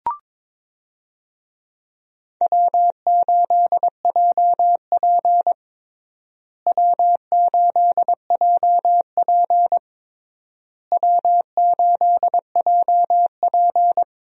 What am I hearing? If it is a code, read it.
Morse code, heard as a keyed tone, about 22 wpm, 700 Hz. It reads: W8JP W8JP W8JP
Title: USA Call Signs - 4 Characters - Repeated 3x - 22wpm